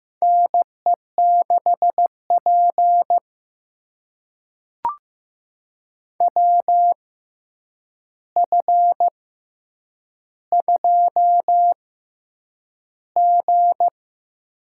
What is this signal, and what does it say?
Morse code, heard as a keyed tone, about 15 wpm, spaced as for 8 wpm, 700 Hz. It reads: NE6P WF2G
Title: USA Call Signs - 4 Characters - 6x Character Spacing - 15wpm